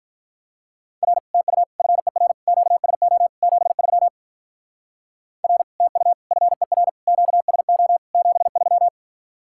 Morse code, heard as a keyed tone, about 44 wpm, 700 Hz. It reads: R TU FER QSO 73 R TU FER QSO 73
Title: Speed-Racing Common QSO Phrases - 35wpm